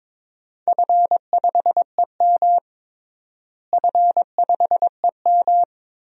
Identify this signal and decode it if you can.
Morse code, heard as a keyed tone, about 22 wpm, 700 Hz. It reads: F5EM F5EM
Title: DXCC-100 Call Signs - 4 Characters - Repeated 3x - 22wpm